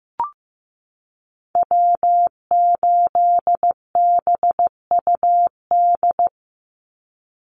Morse code, heard as a keyed tone, about 15 wpm, 700 Hz. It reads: W8BUD
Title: Call Signs Repeated 2x - 15wpm